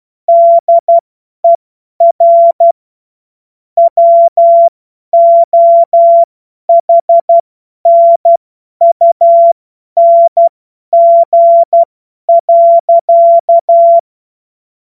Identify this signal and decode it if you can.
Morse code, heard as a keyed tone, about 12 wpm, spaced as for 10 wpm, 680 Hz. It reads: DER WOHNUNG.
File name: Morsekurs / Witze / Witz 07. Juni 2022 50 F70